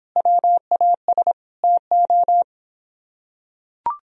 Morse code, heard as a keyed tone, about 26 wpm, 700 Hz. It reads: WAS TO